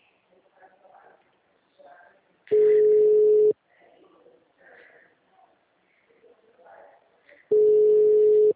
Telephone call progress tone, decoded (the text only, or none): ringback tone